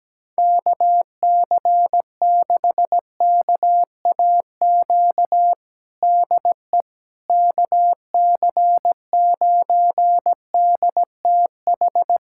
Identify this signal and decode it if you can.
Morse code, heard as a keyed tone, about 17 wpm, 700 Hz. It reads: KC6KAQ DE KC9DTH
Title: USA Callsign DE Callsign - 6 Characters - 17wpm